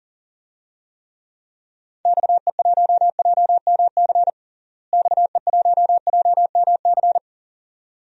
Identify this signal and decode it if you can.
Morse code, heard as a keyed tone, about 37 wpm, 700 Hz. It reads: XE1JMC XE1JMC
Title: DXCC-100 Call Signs - 6 Characters - Repeated 2x - 40wpm